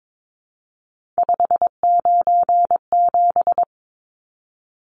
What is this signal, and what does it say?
Morse code, heard as a keyed tone, about 22 wpm, 700 Hz. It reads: 597
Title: Numbers - 3-Digit 22wpm